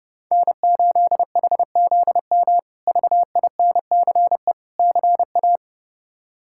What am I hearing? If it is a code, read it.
Morse code, heard as a keyed tone, about 30 wpm, 700 Hz. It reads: N8HZM VINCE CA
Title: K1USN SST QSO Practice - 30wpm